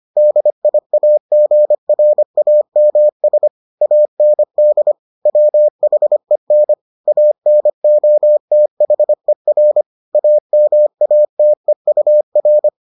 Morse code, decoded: DIAGRAMS AND WHEN ANOTHER AMATEUR

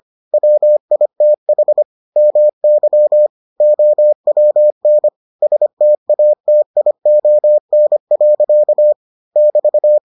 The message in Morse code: WITH MY OWN STATION. <BT>

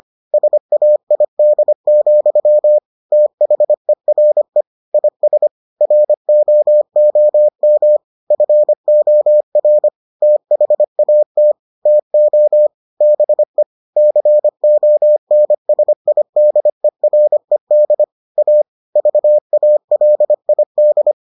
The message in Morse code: SAID, THERE IS ROOM FOR THAT TO BE CONSIDERED A VALID

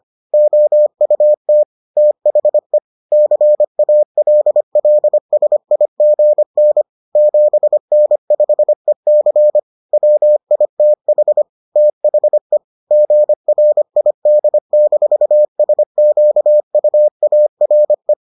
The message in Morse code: OUT THE CALLSIGN 7N5EC WITH THE GRID-SQUARE

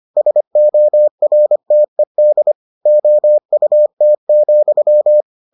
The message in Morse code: SORTED OUT,

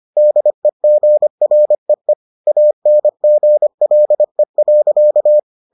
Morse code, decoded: DEGREE ANGLE.

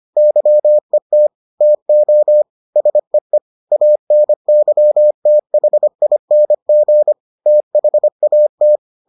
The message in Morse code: YET TO SEE ANYTHING THAT

